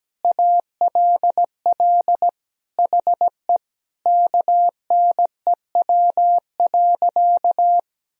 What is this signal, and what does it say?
Morse code, heard as a keyed tone, about 17 wpm, 700 Hz. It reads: ALL HE KNEW.
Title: Sentences from Top 400 Words and Abbreviations 17wpm